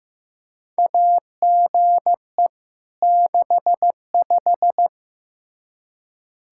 Morse code, heard as a keyed tone, about 15 wpm, 700 Hz. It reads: AGE 65
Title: K1USN SST QSO Practice 15wpm